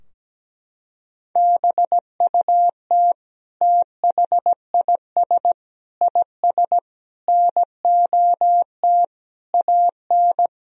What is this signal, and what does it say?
Morse code, heard as a keyed tone, about 17 wpm, 700 Hz. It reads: BUT THIS IS NOT AN
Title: Sentences from Top 1000 Words - Review Part 2 - 17wpm